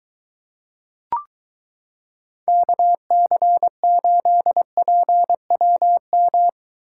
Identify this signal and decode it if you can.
Morse code, heard as a keyed tone, about 22 wpm, 700 Hz. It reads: KC8PWM